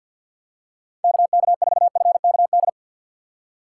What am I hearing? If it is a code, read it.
Morse code, heard as a keyed tone, about 52 wpm, 700 Hz. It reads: KK4FKD
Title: USA Call Signs - 6 Characters - 6x Character Spacing - 50wpm